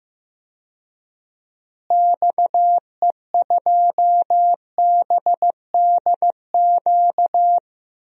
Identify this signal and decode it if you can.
Morse code, heard as a keyed tone, about 15 wpm, 700 Hz. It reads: XE2BDQ
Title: DXCC-100 Call Signs - 6 Characters - Repeated 2x - 15wpm